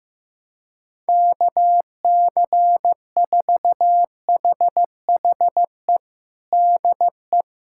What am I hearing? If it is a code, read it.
Morse code, heard as a keyed tone, about 15 wpm, 700 Hz. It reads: KC4HHE DE
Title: USA Callsign DE Callsign - 6 Characters - 15wpm